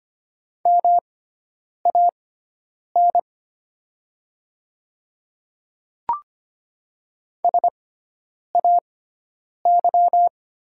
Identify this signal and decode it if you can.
Morse code, heard as a keyed tone, about 25 wpm, spaced as for 9 wpm, 700 Hz. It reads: MAN SAY
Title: 6x Character Spacing - Top 300 Words - 25wpm